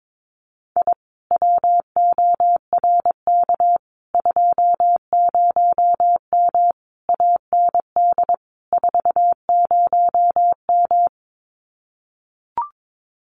Morse code, heard as a keyed tone, about 22 wpm, 700 Hz. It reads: I WORK 20M AND 40M
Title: Common QSO Phrases Encore 22wpm